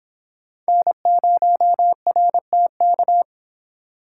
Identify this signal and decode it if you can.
Morse code, heard as a keyed tone, about 26 wpm, 700 Hz. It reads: N0RTK